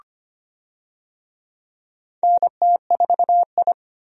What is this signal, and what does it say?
Morse code, heard as a keyed tone, about 25 wpm, 700 Hz. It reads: NT4I
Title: USA Call Signs - 4 Characters - 25wpm